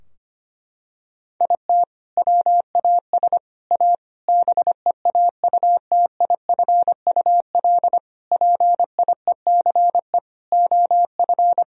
Morse code, heard as a keyed tone, about 25 wpm, 700 Hz. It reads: IT WAS A BEAUTIFUL PIECE OF
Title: Sentences from Top 1000 Words - Review Part 2 - 25wpm